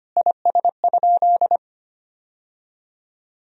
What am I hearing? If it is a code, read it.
Morse code, heard as a keyed tone, about 25 wpm, 700 Hz. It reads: IS?
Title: Sentences from Top 1000 Words - Review Part 1 - 25wpm